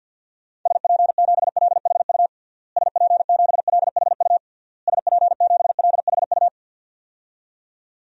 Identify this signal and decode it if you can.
Morse code, heard as a keyed tone, about 52 wpm, 700 Hz. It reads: SP7LHU SP7LHU SP7LHU
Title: DXCC-100 Call Signs - 6 Characters - Repeated 3x - 50wpm